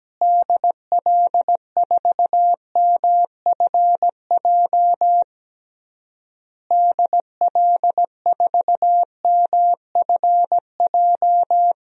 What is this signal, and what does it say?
Morse code, heard as a keyed tone, about 17 wpm, 700 Hz. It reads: DL4MFJ DL4MFJ
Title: DXCC-100 Call Signs - 6 Characters - Repeated 3x - 17wpm